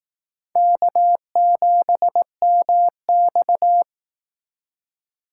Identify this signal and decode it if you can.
Morse code, heard as a keyed tone, about 19 wpm, 700 Hz. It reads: K7MX
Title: Speed Racing - USA 1x2 Call Signs 15wpm